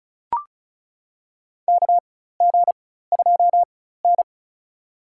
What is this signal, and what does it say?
Morse code, heard as a keyed tone, about 36 wpm, spaced as for 16 wpm, 700 Hz. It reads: KG2N
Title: USA Call Signs - 4x Character Spacing - 35wpm